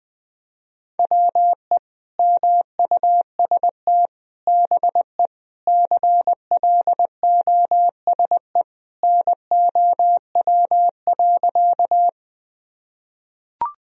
Morse code, heard as a keyed tone, about 20 wpm, 700 Hz. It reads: WE MUST BE CLOSE NOW.
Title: Sentences from Top 1000 Words - Review Part 1 - 20wpm